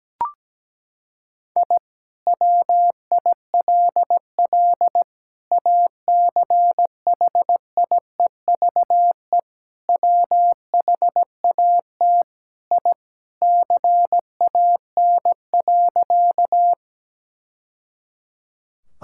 Morse code, heard as a keyed tone, about 17 wpm, 700 Hz. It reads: I WILL ACHIEVE WHAT I CAN.
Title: Sentences from Top 1000 Words - Review Part 6 - 17wpm